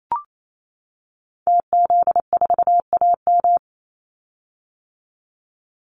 Morse code, heard as a keyed tone, about 28 wpm, 700 Hz. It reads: TZ4AM
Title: Call Signs - 28wpm